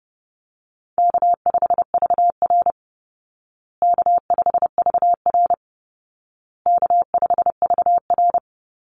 Morse code, heard as a keyed tone, about 30 wpm, 700 Hz. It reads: K5VR K5VR K5VR
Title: Easy Call Signs Repeated 3x - 30wpm